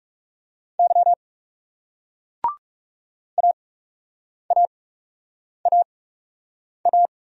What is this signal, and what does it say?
Morse code, heard as a keyed tone, about 40 wpm, spaced as for 8 wpm, 700 Hz. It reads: Y AAAA